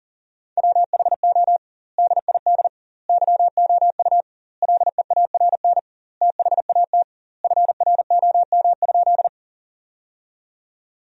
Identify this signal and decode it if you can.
Morse code, heard as a keyed tone, about 40 wpm, 700 Hz. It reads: WHO DID YOU LEARN THAT FROM?
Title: 2x Word-Spacing - Sentences from Top 300 Words - 40wpm